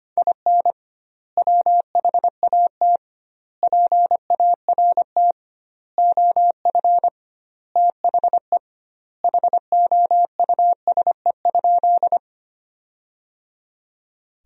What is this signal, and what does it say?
Morse code, heard as a keyed tone, about 25 wpm, 700 Hz. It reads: IN WHAT PART OF THE HOUSE?
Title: 2x Word-Spacing - Sentences from Top 200 Words - 25wpm